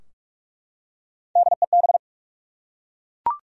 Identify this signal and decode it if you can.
Morse code, heard as a keyed tone, about 47 wpm, 700 Hz. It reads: DEB
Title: USA Common Names - Max 5 Letters - 45wpm